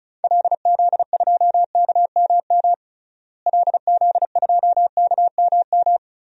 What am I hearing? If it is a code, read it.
Morse code, heard as a keyed tone, about 36 wpm, 700 Hz. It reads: LZ2KMM LZ2KMM
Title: DXCC-100 Call Signs - 6 Characters - Repeated 2x - 35wpm